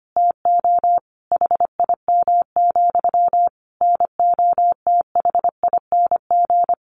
Morse code, heard as a keyed tone, about 25 wpm, 700 Hz. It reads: TO HIM, NOTHING
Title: Sentences from Top 1300 Words 25wpm